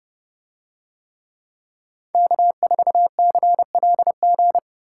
Morse code, heard as a keyed tone, about 29 wpm, 700 Hz. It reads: K4CLG